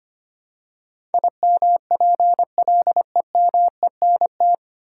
Morse code, heard as a keyed tone, about 25 wpm, 700 Hz. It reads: IMPLEMENT
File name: practice / 25wpm / 9 Letter Words - First 3 Letters Given - 25wpm